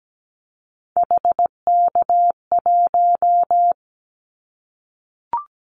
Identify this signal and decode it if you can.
Morse code, heard as a keyed tone, about 17 wpm, 700 Hz. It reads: HK1
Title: Callsign Prefixes for Top 100 DXCC - 17wpm